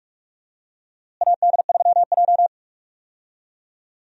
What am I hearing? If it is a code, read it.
Morse code, heard as a keyed tone, about 41 wpm, 700 Hz. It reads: AD3J
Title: USA Call Signs - 4 Characters - Repeated 3x - 45wpm